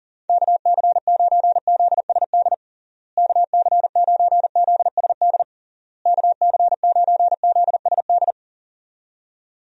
Morse code, heard as a keyed tone, about 40 wpm, 700 Hz. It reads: KC9ZSD KC9ZSD KC9ZSD
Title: USA Call Signs - 6 Characters - Repeated 3x - 40wpm